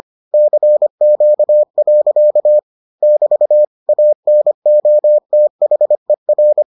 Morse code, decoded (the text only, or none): CQ. <BT> ANOTHER